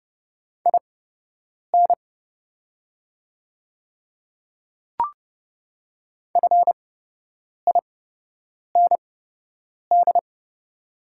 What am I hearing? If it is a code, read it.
Morse code, heard as a keyed tone, about 30 wpm, spaced as for 8 wpm, 690 Hz. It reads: IN FIND
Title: 8x Character Spacing - Top 200 Words - 30wpm